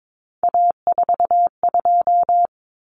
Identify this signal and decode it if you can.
Morse code, heard as a keyed tone, about 22 wpm, 700 Hz. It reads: A42
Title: Common Cut Numbers - 3-Digit 22wpm